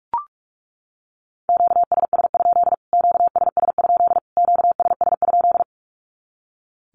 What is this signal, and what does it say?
Morse code, heard as a keyed tone, about 44 wpm, 700 Hz. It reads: QSS? QSS? QSS?
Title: Q-Codes Repeated 3x 45wpm